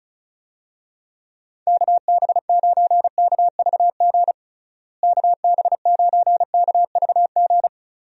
Morse code, heard as a keyed tone, about 34 wpm, 700 Hz. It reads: KB9KVG KB9KVG
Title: USA Call Signs - 6 Characters - Repeated 3x - 35wpm